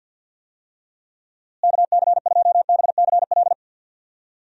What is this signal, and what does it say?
Morse code, heard as a keyed tone, about 49 wpm, 700 Hz. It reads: KC2BCL
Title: USA Call Signs - 6 Characters - Repeated 2x - 50wpm